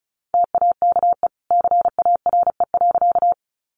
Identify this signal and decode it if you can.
Morse code, heard as a keyed tone, about 34 wpm, 700 Hz. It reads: TAKE CARE.
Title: Sentences from Top 300 Words 35wpm